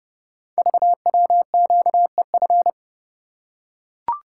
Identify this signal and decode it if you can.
Morse code, heard as a keyed tone, about 30 wpm, 700 Hz. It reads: VWQEF
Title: Code Groups - 5 Letters - 30wpm